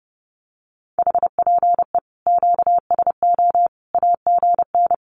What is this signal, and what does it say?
Morse code, heard as a keyed tone, about 28 wpm, 700 Hz. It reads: HPE QSO AGN
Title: Common QSO Phrases 30wpm